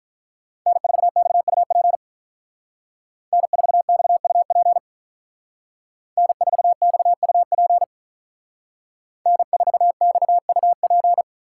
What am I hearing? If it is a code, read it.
Morse code, heard as a keyed tone, about 45 wpm, 700 Hz. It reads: N4XUP N4XUP N4XUP N4XUP